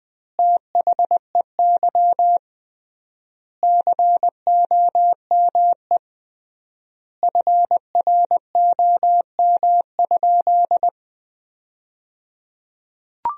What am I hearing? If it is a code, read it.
Morse code, heard as a keyed tone, about 21 wpm, 700 Hz. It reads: THEY COME FROM?
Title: 3x Word-Spacing - Sentences from Top 200 Words - 20wpm